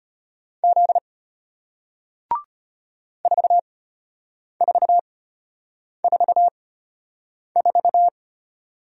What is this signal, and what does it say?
Morse code, heard as a keyed tone, about 34 wpm, spaced as for 8 wpm, 700 Hz. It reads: Z 4444